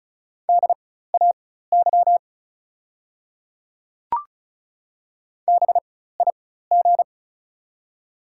Morse code, heard as a keyed tone, about 36 wpm, spaced as for 16 wpm, 700 Hz. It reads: DAY BIG
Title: 4x Character Spacing - Top 200 Words - 35wpm